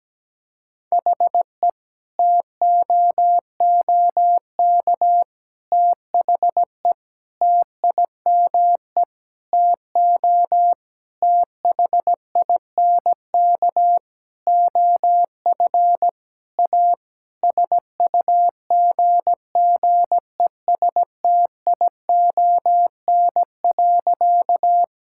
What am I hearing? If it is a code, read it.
Morse code, heard as a keyed tone, about 17 wpm, 700 Hz. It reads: HE TOOK THE TIME TO THINK OF A SUGGESTION.
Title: Sentences from Top 1600 Words 17wpm